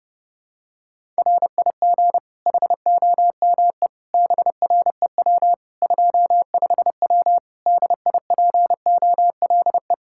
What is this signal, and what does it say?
Morse code, heard as a keyed tone, about 30 wpm, 700 Hz. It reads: RIG HOME BREW 25W DIPOLE
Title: K1USN SST QSO Practice 30wpm